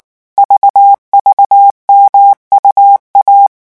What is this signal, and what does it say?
Morse code, heard as a keyed tone, about 19 wpm, 800 Hz. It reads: VVMUA
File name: morse mp3 / revision2-vitesse 18 mots